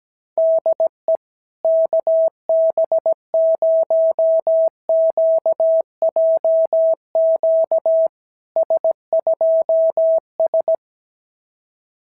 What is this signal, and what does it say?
Morse code, heard as a keyed tone, about 17 wpm, 660 Hz. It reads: DE KB0QJQ S2S
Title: SOTA QSO Practice 17wpm